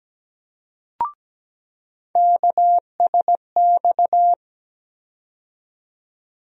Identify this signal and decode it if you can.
Morse code, heard as a keyed tone, about 17 wpm, 700 Hz. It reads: KSX
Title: USA Call Sign Postfixes - 3 Letters 17wpm